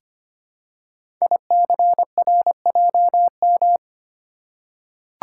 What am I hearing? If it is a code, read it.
Morse code, heard as a keyed tone, about 25 wpm, 700 Hz. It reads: ICRJM
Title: Code Groups - 5 Letters - 25wpm